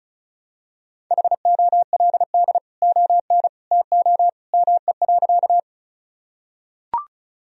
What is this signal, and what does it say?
Morse code, heard as a keyed tone, about 36 wpm, 700 Hz. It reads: HOLD ON TO ME.
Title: Sentences from Top 1000 Words - Review Part 2 - 35wpm